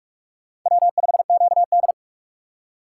Morse code, heard as a keyed tone, about 40 wpm, 700 Hz. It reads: W5QD